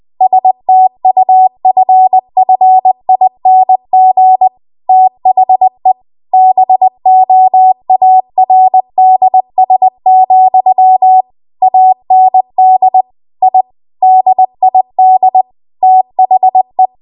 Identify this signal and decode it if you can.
Morse code, heard as a keyed tone, about 20 wpm, 750 Hz. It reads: STUFFING THE BOARDS, AND I DID THE